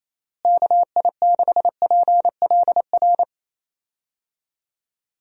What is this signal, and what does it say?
Morse code, heard as a keyed tone, about 27 wpm, 700 Hz. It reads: KI6PLR
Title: USA Call Signs - 6 Characters - Repeated 2x - 28wpm